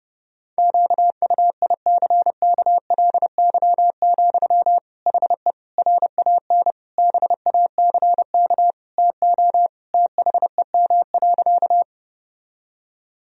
Morse code, heard as a keyed tone, about 32 wpm, 700 Hz. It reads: QUICKLY, HE RAN BACK TO THEM.
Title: Sentences from Top 1000 Words - Review Part 3 - 30wpm